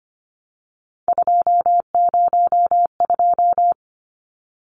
Morse code, heard as a keyed tone, about 25 wpm, 700 Hz. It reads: 202
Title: Numbers - 3-Digit 25wpm